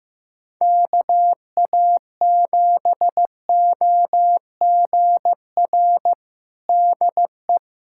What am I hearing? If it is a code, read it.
Morse code, heard as a keyed tone, about 16 wpm, 700 Hz. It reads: KA7OGR DE
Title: USA Callsign DE Callsign - 6 Characters - 15wpm